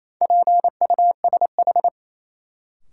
Morse code, heard as a keyed tone, about 27 wpm, 700 Hz. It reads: PUSH
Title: Sentences from Top 1300 Words - New Word Drill - 28wpm